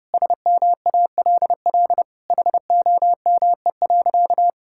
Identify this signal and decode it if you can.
Morse code, heard as a keyed tone, about 30 wpm, 700 Hz. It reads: SMALL HOME.
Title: Sentences from Top 1000 Words - Review Part 6 - 30wpm